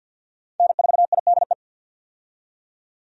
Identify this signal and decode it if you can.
Morse code, heard as a keyed tone, about 50 wpm, 700 Hz. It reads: N4IDE
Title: USA Call Signs - 5 Characters - 50wpm